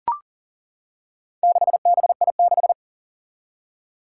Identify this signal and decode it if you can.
Morse code, heard as a keyed tone, about 40 wpm, 700 Hz. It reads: 6BI6
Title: Four Challenging Letter-Number Combinations 40wpm